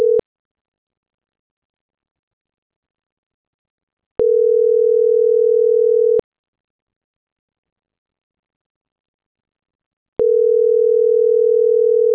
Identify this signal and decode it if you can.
Ringback tone (North America), 440 + 480 Hz.
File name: CENTRAL MABE / out luis.palacios - .79543